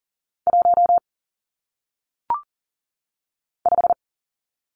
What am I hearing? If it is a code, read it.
Morse code, heard as a keyed tone, about 39 wpm, 700 Hz. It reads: 1 5
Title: Numbers - 1-Digit 40wpm